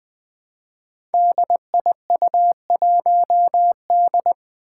Morse code, heard as a keyed tone, about 20 wpm, 700 Hz. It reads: DIU1D
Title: Five Challenging Letter-Number Combinations 20wpm